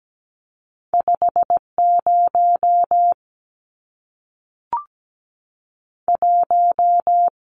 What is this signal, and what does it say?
Morse code, heard as a keyed tone, about 17 wpm, 700 Hz. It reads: 5 0 1
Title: Numbers - 2-Digit 17wpm